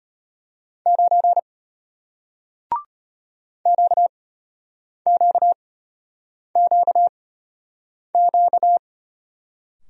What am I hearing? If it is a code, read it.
Morse code, heard as a keyed tone, about 34 wpm, spaced as for 8 wpm, 700 Hz. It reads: 9 QQQQ